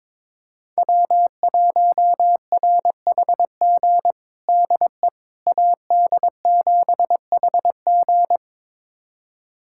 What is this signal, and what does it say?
Morse code, heard as a keyed tone, about 22 wpm, 700 Hz. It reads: W1RHG DE AD7HG
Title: USA Callsign DE Callsign - 5 Characters - 22wpm